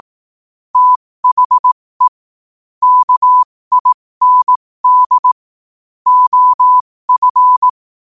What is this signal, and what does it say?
Morse code, heard as a keyed tone, about 18 wpm, spaced as for 15 wpm, 1000 Hz. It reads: THE KIND OF
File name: qotdfiles / 15wpm